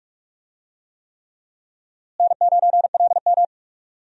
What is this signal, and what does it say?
Morse code, heard as a keyed tone, about 46 wpm, 700 Hz. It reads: N9LM
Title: USA Call Signs - 4 Characters - Repeated 3x - 45wpm